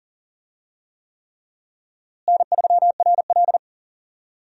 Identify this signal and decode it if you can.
Morse code, heard as a keyed tone, about 40 wpm, 700 Hz. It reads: N3RL
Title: USA Call Signs - 4 Characters - Repeated 3x - 40wpm